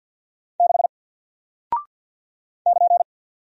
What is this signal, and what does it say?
Morse code, heard as a keyed tone, about 48 wpm, 700 Hz. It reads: 6 <KN>
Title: Single Character (Letter:Number:Punctuation:Prosign) Alt 50wpm